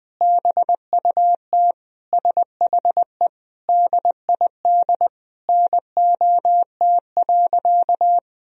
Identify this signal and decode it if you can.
Morse code, heard as a keyed tone, about 20 wpm, 700 Hz. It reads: BUT SHE DID NOT.
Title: Sentences from Top 1000 Words - Review Part 1 - 20wpm